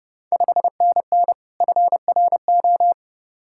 Morse code, heard as a keyed tone, about 30 wpm, 700 Hz. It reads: 5NN FRO
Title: Ontario QSO Party Exchange 30wpm